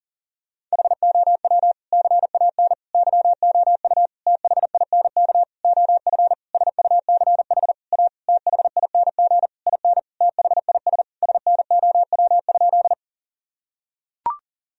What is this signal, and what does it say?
Morse code, heard as a keyed tone, about 40 wpm, 700 Hz. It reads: HOW CAN YOU THINK OF SUCH A THING IN THIS SNOW?
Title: Sentences from Top 1000 Words - Review Part 3 - 40wpm